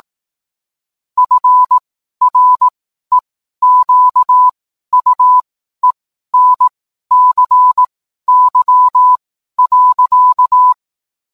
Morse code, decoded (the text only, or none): FREQUENCY.